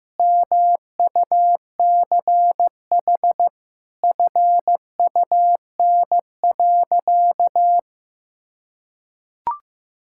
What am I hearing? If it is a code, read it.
Morse code, heard as a keyed tone, about 15 wpm, 700 Hz. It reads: MUCH FUN.
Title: Sentences from Top 1600 Words 15wpm